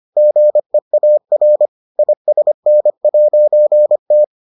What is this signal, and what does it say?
Morse code, heard as a keyed tone, about 25 wpm, 600 Hz. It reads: GEAR ISN'T